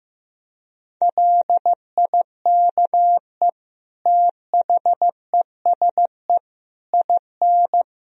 Morse code, heard as a keyed tone, about 16 wpm, 700 Hz. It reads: LIKE THESE IN
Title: Sentences from Top 500 Words and Abbreviations 15wpm